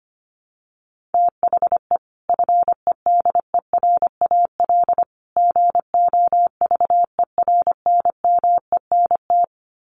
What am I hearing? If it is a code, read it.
Morse code, heard as a keyed tone, about 25 wpm, 700 Hz. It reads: THE FEDERAL GOVERNMENT
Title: Sets of 3 Words 25wpm